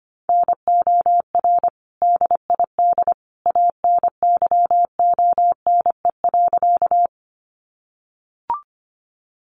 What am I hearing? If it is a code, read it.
Morse code, heard as a keyed tone, about 25 wpm, 700 Hz. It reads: NOR DID ANYONE.
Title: Sentences from Top 1500 Words 25wpm